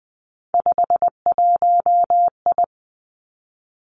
Morse code, heard as a keyed tone, about 20 wpm, 700 Hz. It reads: 51I
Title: Three Challenging Letter-Number Combinations 20wpm